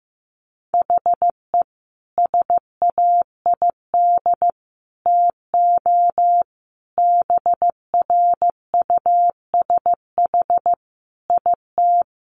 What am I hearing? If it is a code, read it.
Morse code, heard as a keyed tone, about 15 wpm, 700 Hz. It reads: HE SAID TO BRUSH IT
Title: Sentences from Top 1100 Words 15wpm